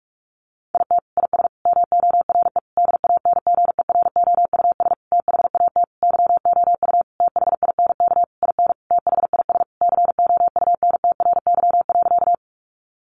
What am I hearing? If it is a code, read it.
Morse code, heard as a keyed tone, about 44 wpm, 700 Hz. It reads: IT IS MORE DANGEROUS THAT YOU THINK IN THIS COUNTRY.
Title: Sentences from Top 1200 Words 45wpm